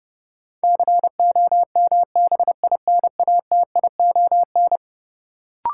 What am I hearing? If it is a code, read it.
Morse code, heard as a keyed tone, about 31 wpm, 700 Hz. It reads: COMBINATION
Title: Sentences from Top 1300 Words - New Word Drill Alt - 30wpm